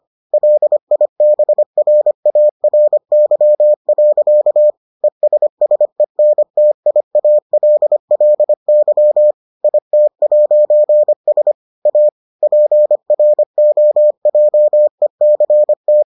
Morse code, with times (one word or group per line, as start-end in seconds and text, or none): 0.3-4.7 LIBRARY.
5.0-9.3 ESSENTIALLY
9.6-11.5 IT'S
11.9-12.1 A
12.4-16.0 PROJECT